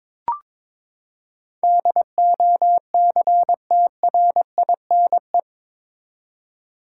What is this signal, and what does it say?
Morse code, heard as a keyed tone, about 22 wpm, 700 Hz. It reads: DOCTRINE
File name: practice / 22wpm / Eight Letter Words - 22wpm